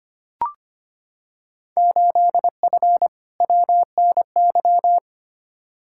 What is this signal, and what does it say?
Morse code, heard as a keyed tone, about 25 wpm, 700 Hz. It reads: 8F WNY
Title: Field Day Exchanges Encore - 25wpm